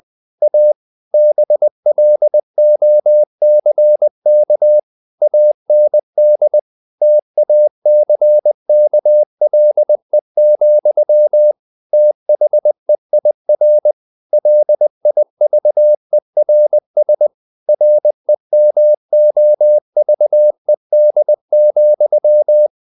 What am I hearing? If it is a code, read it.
Morse code, heard as a keyed tone, about 20 wpm, 600 Hz. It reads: A BLOCK AND TACKLE, THEIR LIVERS REMOVED,
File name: the old man and the sea / the old man and the sea part 1 20wpm